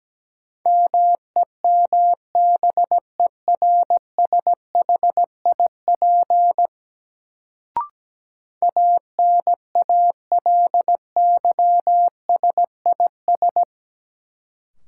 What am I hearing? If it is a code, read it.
Morse code, heard as a keyed tone, about 17 wpm, 700 Hz. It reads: MEMBERSHIP ANALYSIS
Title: Sentences from Top 1400 Words - New Word Drill Alt - 17wpm